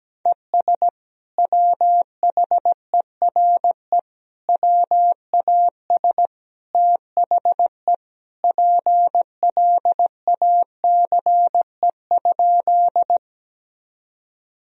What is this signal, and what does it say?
Morse code, heard as a keyed tone, about 17 wpm, 700 Hz. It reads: ES WHERE WAS THE PLACE?
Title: Sentences from Top 200 Words and Abbreviations 17wpm